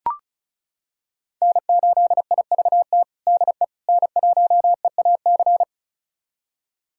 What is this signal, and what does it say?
Morse code, heard as a keyed tone, about 34 wpm, 700 Hz. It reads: N8IVT DE N1EAC
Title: USA Callsign DE Callsign - 5 Characters - 35wpm